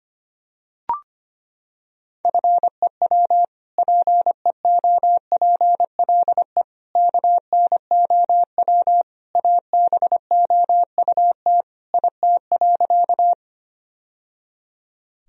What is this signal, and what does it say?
Morse code, heard as a keyed tone, about 25 wpm, 700 Hz. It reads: FEW PEOPLE KNOW ABOUT IT.
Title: Sentences from Top 1000 Words - Review Part 1 - 25wpm